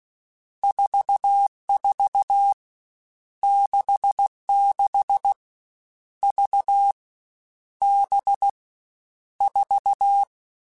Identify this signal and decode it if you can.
Morse code, heard as a keyed tone, about 16 wpm, 800 Hz. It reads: 44 66 V B 4
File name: Groups@13 / Groups@13 mp3s / VB46@13